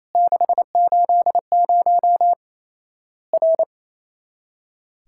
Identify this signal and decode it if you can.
Morse code, heard as a keyed tone, about 28 wpm, 700 Hz. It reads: 680 R
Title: Worked All Europe DX Contest - QTC Exchange - 28wpm